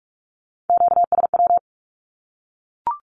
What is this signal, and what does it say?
Morse code, heard as a keyed tone, about 44 wpm, 700 Hz. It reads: QSW
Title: Q-Codes 45wpm